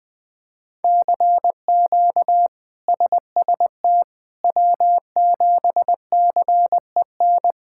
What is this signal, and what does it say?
Morse code, heard as a keyed tone, about 21 wpm, 700 Hz. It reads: CQ SST W7CEN
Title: K1USN SST QSO Practice - 20wpm